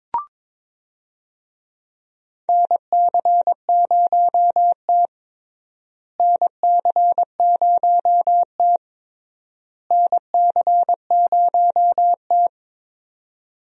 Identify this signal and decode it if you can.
Morse code, heard as a keyed tone, about 22 wpm, 700 Hz. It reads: NC0T NC0T NC0T
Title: USA Call Signs - 4 Characters - Repeated 3x - 22wpm